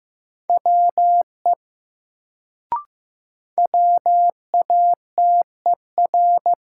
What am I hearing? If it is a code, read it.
Morse code, heard as a keyed tone, about 15 wpm, 700 Hz. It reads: WE WATER
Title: Sentences from Top 100 Words - New Word Drill - 15wpm